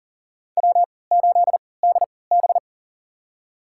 Morse code, heard as a keyed tone, about 40 wpm, spaced as for 23 wpm, 700 Hz. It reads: W8DB
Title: USA Call Signs - 4 Characters - 3x Character Spacing - 40wpm